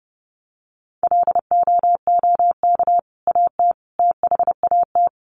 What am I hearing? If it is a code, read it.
Morse code, heard as a keyed tone, about 30 wpm, 700 Hz. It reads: LOOK AT THAT
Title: Sentences from Top 1100 Words 30wpm